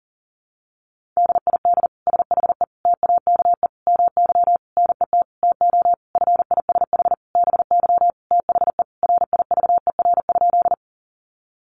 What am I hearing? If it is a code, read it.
Morse code, heard as a keyed tone, about 40 wpm, 700 Hz. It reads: DID SHE TAKE MY NET TO FISH BY THE RIVER?
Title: Sentences from Top 1400 Words 40wpm